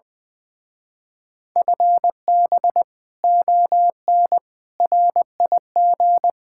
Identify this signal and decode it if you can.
Morse code, heard as a keyed tone, about 19 wpm, 700 Hz. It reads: FB ON RIG
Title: Speed-Racing Common QSO Phrases - 20wpm